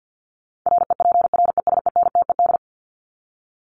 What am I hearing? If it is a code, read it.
Morse code, heard as a keyed tone, about 50 wpm, 700 Hz. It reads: REPRESENTED
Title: ED Suffix - Top 100 Words with Base Words 50wpm